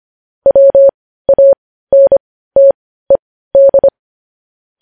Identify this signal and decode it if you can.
Morse code, heard as a keyed tone, about 25 wpm, spaced as for 15 wpm, 550 Hz. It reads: WANTED
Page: news headlines in morse code at 15 wpm